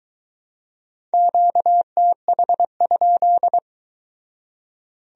Morse code, heard as a keyed tone, about 22 wpm, 700 Hz. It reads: QTH?